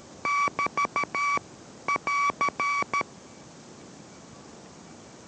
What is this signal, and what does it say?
Morse code, heard as a keyed tone, about 13 wpm, 1140 Hz. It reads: <BT> <AR>